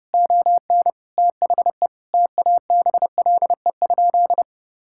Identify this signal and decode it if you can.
Morse code, heard as a keyed tone, about 30 wpm, 700 Hz. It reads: ON THE TABLE?
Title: Sentences from Top 1000 Words - Review Part 5 - 30wpm